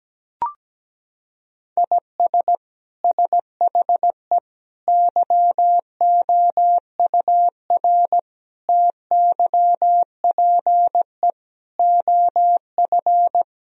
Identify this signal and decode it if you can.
Morse code, heard as a keyed tone, about 17 wpm, 700 Hz. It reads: IS SHE YOUR TYPE OF
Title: Sentences from Top 1000 Words - Review Part 3 - 17wpm